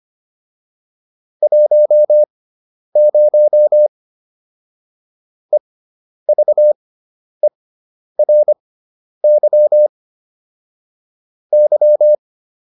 Morse code, heard as a keyed tone, about 25 wpm, spaced as for 10 wpm, 600 Hz. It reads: 10 EVERY Y